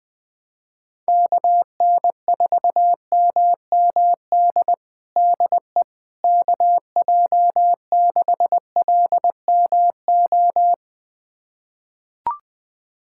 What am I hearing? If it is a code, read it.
Morse code, heard as a keyed tone, about 20 wpm, 700 Hz. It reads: KN4MMD DE KJ6LMO
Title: USA Callsign DE Callsign - 6 Characters - 20wpm